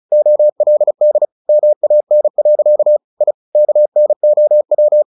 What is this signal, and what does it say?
Morse code, heard as a keyed tone, about 34 wpm, 600 Hz. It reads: OLD MAN. I KNOW
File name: the old man and the sea 2 35wpm